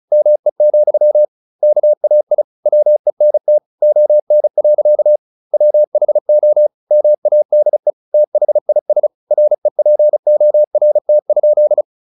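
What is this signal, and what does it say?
Morse code, heard as a keyed tone, about 34 wpm, 600 Hz. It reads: ME, KAI WENT ON. WHO MADE THIS REPORT?
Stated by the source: Morse practice at 35 wpm from CW med Gnister, sending Ken Follett's novel Never in English.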